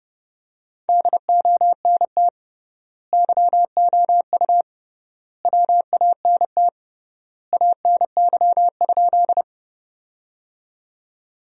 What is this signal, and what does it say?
Morse code, heard as a keyed tone, about 30 wpm, 700 Hz. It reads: DONT YOU WANT ANY?
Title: 3x Word-Spacing - Sentences from Top 300 Words - 30wpm